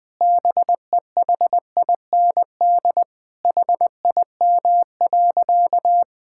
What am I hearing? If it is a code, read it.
Morse code, heard as a keyed tone, about 20 wpm, 700 Hz. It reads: BEHIND HIM.
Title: Sentences from Top 1600 Words 20wpm